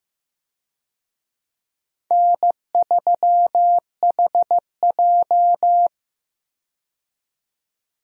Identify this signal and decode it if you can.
Morse code, heard as a keyed tone, about 15 wpm, 700 Hz. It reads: N3HJ
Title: USA Call Signs - 4 Characters - 15wpm